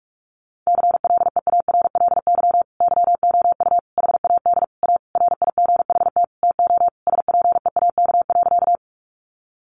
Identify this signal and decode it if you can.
Morse code, heard as a keyed tone, about 46 wpm, 700 Hz. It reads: CLEARLY YOU HAD A RIGHT TO SPEAK.
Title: Sentences from Top 1300 Words 45wpm